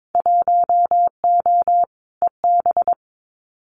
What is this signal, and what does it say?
Morse code, heard as a keyed tone, about 22 wpm, 700 Hz. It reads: 1O EB
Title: Winter Field Day Exchanges 22wpm